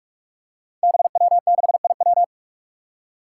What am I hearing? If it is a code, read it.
Morse code, heard as a keyed tone, about 45 wpm, 700 Hz. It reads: BW6IW
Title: Five Challenging Letter-Number Combinations 45wpm